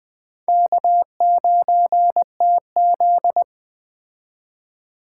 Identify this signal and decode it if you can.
Morse code, heard as a keyed tone, about 21 wpm, 700 Hz. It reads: K9TZ